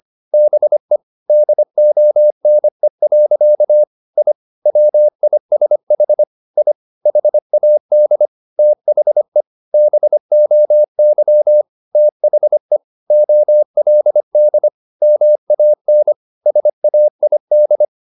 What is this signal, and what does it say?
Morse code, heard as a keyed tone, about 25 wpm, 600 Hz. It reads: BE DONE. I WISH I HAD THE BOY THE OLD MAN SAID
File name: the old man and the sea part 1 25wpm